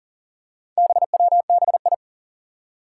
Morse code, heard as a keyed tone, about 38 wpm, 700 Hz. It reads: BWBI
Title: Four Challenging Letter-Number Combinations 40wpm